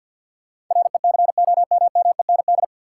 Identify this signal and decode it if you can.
Morse code, heard as a keyed tone, about 53 wpm, 700 Hz. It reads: RECOMMEND
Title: Sentences from Top 1200 Words - New Word Drill - 50wpm